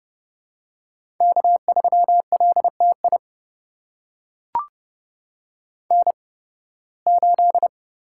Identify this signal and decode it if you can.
Morse code, heard as a keyed tone, about 30 wpm, spaced as for 20 wpm, 700 Hz. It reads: K3LTI N8
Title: USA Call Signs - 5 Characters - 8x Character Spacing - 30wpm